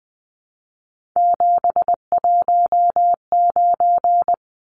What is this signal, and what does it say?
Morse code, heard as a keyed tone, about 20 wpm, 700 Hz. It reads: 719
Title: Numbers - 3-Digit 20wpm